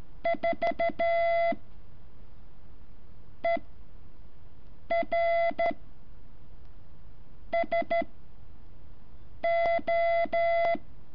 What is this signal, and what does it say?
Morse code, heard as a keyed tone, about 13 wpm, spaced as for 5 wpm, 680 Hz. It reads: 4 ER SO
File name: curso / ejercicio 14